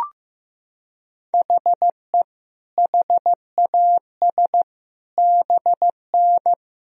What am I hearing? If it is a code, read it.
Morse code, heard as a keyed tone, about 15 wpm, 700 Hz. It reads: HE HAS BN
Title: Sentences from Top 300 Words and Abbreviations 15wpm